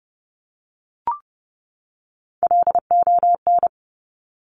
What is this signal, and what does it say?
Morse code, heard as a keyed tone, about 30 wpm, 700 Hz. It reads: LON
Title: Top Three Letter Combinations in Words 30wpm